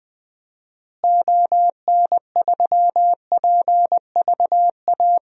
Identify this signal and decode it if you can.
Morse code, heard as a keyed tone, about 20 wpm, 700 Hz. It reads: ON3PVA
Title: DXCC-100 Call Signs - 6 Characters - Repeated 2x - 20wpm